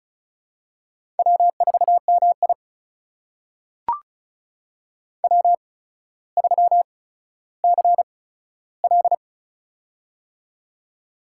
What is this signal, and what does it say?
Morse code, heard as a keyed tone, about 36 wpm, spaced as for 15 wpm, 700 Hz. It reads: W4MI W3CL
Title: USA Call Signs - 4 Characters - 8x Character Spacing - 35wpm